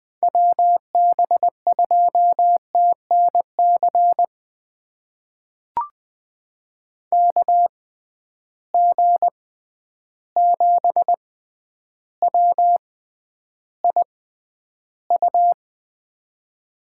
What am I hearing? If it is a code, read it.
Morse code, heard as a keyed tone, about 20 wpm, spaced as for 10 wpm, 700 Hz. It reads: WB2TNC KG7WIU
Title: USA Call Signs - 6 Characters - 6x Character Spacing - 20wpm